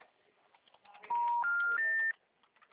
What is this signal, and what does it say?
Special information tone.